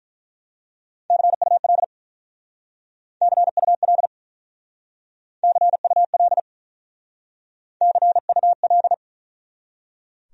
Morse code, heard as a keyed tone, about 42 wpm, 700 Hz. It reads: CUL CUL CUL CUL
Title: Speed-Racing Abbreviations - 35wpm